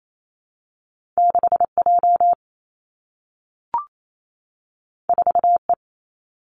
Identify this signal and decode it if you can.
Morse code, heard as a keyed tone, about 28 wpm, 700 Hz. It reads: 6 J 4 E
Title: Two Challenging Letter-Number Combinations 28wpm